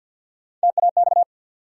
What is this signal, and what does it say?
Morse code, heard as a keyed tone, about 46 wpm, 700 Hz. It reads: TAX
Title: Sentences from Top 1000 Words - Review Part 5 - New Word Drill - 50wpm